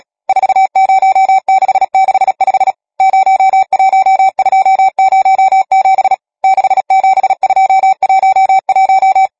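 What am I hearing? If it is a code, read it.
Morse code, heard as a keyed tone, about 37 wpm, 770 Hz. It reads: 40665 01207 67211
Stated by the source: Morse code number practice, machine-sent training text (Nácvik telegrafie, OK1FCB).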